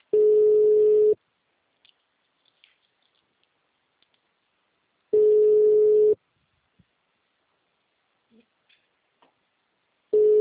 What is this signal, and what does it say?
Ringback tone (Europe, 425 Hz).
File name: out- -2012- - - .54079